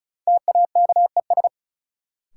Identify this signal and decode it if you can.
Morse code, heard as a keyed tone, about 34 wpm, 700 Hz. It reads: TAKES
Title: Sentences from Top 1000 Words - Review Part 6 - New Word Drill - 35wpm